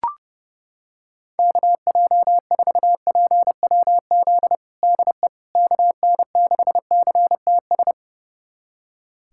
Morse code, heard as a keyed tone, about 30 wpm, 700 Hz. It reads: KJ4PWZ DE KN6CTS
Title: USA Callsign DE Callsign 30wpm